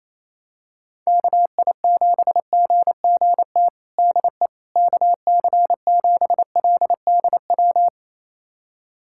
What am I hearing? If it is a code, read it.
Morse code, heard as a keyed tone, about 28 wpm, 700 Hz. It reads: KI7GGT DE KC7LDW
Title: USA Callsign DE Callsign - 6 Characters - 28wpm